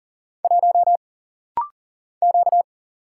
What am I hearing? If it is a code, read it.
Morse code, heard as a keyed tone, about 40 wpm, 700 Hz. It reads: 1 Q
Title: Single Letter-Number - Rapid-Fire 40wpm